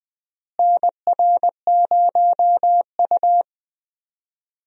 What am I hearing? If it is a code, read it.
Morse code, heard as a keyed tone, about 20 wpm, 700 Hz. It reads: NR0U
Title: USA Call Signs - 4 Characters - Repeated 3x - 20wpm